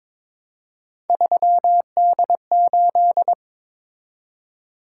Morse code, heard as a keyed tone, about 27 wpm, 700 Hz. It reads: 3D8
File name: Speed-Racing - Three Letter-Number Combinations - 22wpm